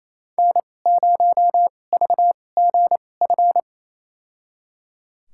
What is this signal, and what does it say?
Morse code, heard as a keyed tone, about 27 wpm, spaced as for 20 wpm, 700 Hz. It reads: N0VGF
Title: USA Call Signs - 2x Character Spacing - 28wpm